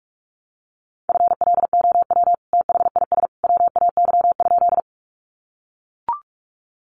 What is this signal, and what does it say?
Morse code, heard as a keyed tone, about 44 wpm, 700 Hz. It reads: FLOW THIS WAY?
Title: Sentences from Top 1200 Words 45wpm